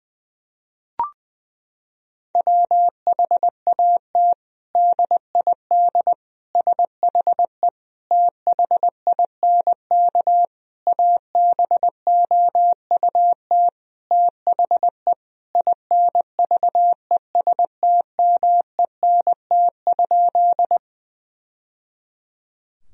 Morse code, heard as a keyed tone, about 21 wpm, 700 Hz. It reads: WHAT DID SHE THINK ABOUT THE INVESTMENT?
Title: Sentences from Top 1000 Words - Review Part 4 - 20wpm